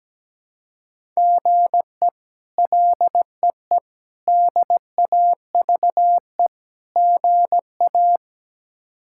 Morse code, heard as a keyed tone, about 17 wpm, 700 Hz. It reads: GE LEE DAVE GA
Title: K1USN SST QSO Practice - 17wpm